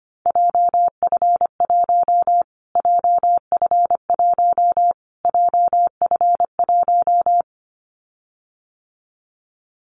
Morse code, heard as a keyed tone, about 25 wpm, 700 Hz. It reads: JF1 JF1 JF1
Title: Callsign Prefixes for Top 25 DXCC - Repeated 3x - 25wpm